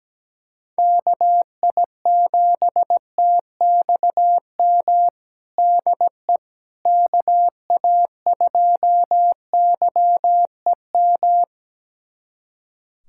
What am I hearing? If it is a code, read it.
Morse code, heard as a keyed tone, about 17 wpm, 700 Hz. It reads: KI7TXM DE KA2YEM
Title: USA Callsign DE Callsign - 6 Characters - 17wpm